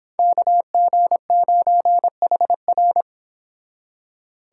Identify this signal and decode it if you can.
Morse code, heard as a keyed tone, about 29 wpm, 700 Hz. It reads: KG9HR